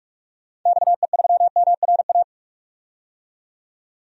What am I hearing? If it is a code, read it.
Morse code, heard as a keyed tone, about 47 wpm, 700 Hz. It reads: XE3MRA
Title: DXCC-100 Call Signs - 6 Characters - 45wpm